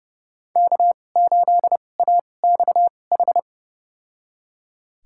Morse code, heard as a keyed tone, about 30 wpm, spaced as for 22 wpm, 700 Hz. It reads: K8AXH
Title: USA Call Signs - 5 Characters - 2x Character Spacing - 30wpm